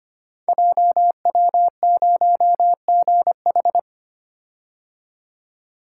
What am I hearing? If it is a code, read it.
Morse code, heard as a keyed tone, about 25 wpm, 690 Hz. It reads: JW0GH
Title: Code Groups - 5 Letter:Numbers - 25wpm